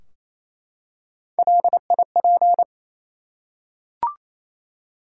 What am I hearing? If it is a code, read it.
Morse code, heard as a keyed tone, about 28 wpm, 700 Hz. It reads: LIP
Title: Three Letter Words - 28wpm